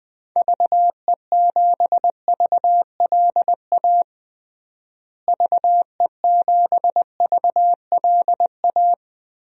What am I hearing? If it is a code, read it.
Morse code, heard as a keyed tone, about 21 wpm, 700 Hz. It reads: VE7VLA VE7VLA
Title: DXCC-100 Call Signs - 6 Characters - Repeated 2x - 20wpm